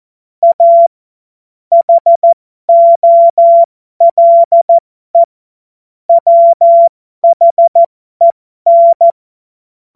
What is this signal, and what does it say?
Morse code, heard as a keyed tone, about 14 wpm, spaced as for 12 wpm, 680 Hz. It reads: A HOLE WHEN
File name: Morsekurs / Witze